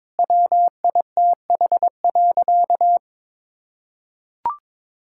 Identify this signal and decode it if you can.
Morse code, heard as a keyed tone, about 22 wpm, 700 Hz. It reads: WITH.
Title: Sentences from Top 1000 Words - Review Part 5 - 22wpm